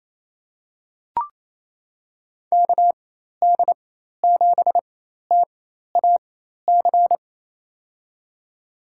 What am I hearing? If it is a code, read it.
Morse code, heard as a keyed tone, about 28 wpm, spaced as for 13 wpm, 700 Hz. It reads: KD7TAC
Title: USA Call Signs - 6 Characters - 4x Character Spacing - 28wpm